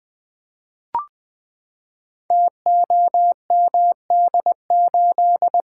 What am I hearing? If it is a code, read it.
Morse code, heard as a keyed tone, about 20 wpm, 700 Hz. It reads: TOMD8
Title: Code Groups - 5 Letter:Numbers - 20wpm